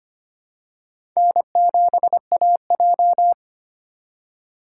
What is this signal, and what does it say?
Morse code, heard as a keyed tone, about 25 wpm, 700 Hz. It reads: N7AJ
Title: USA Call Signs - 4x Character Spacing - 25wpm